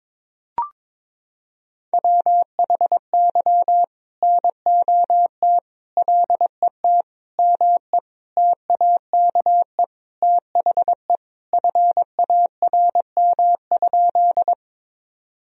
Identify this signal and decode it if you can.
Morse code, heard as a keyed tone, about 22 wpm, 700 Hz. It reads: WHY NOT LET ME TAKE THE FARM?
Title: Sentences from Top 1000 Words - Review Part 2 - 22wpm